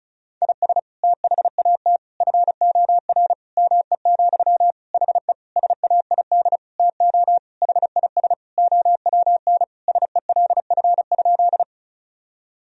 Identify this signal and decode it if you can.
Morse code, heard as a keyed tone, about 34 wpm, 700 Hz. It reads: IS THAT FOR ME, HE SAID TO HIS OWN SELF?
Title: Sentences from Top 1000 Words - Review Part 1 - 35wpm